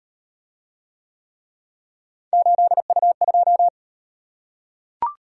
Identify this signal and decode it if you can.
Morse code, heard as a keyed tone, about 36 wpm, 700 Hz. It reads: 8U2